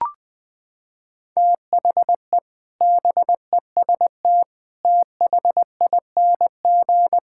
Morse code, heard as a keyed tone, about 20 wpm, 700 Hz. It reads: THE BEST THING
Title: Sets of 8 Words - 20wpm